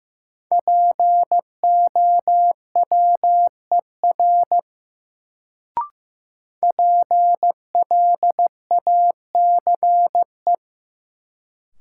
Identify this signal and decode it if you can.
Morse code, heard as a keyed tone, about 15 wpm, 700 Hz. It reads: POWER PLACE
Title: Sets of 4 Words - New Word Drill - 15wpm